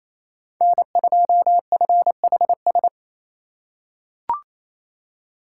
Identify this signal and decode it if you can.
Morse code, heard as a keyed tone, about 28 wpm, 700 Hz. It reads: N2FHS
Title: USA Call Signs - 5 Characters - 6x Character Spacing - 28wpm